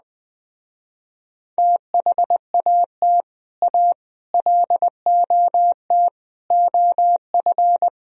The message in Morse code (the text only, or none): THAT A LOT OF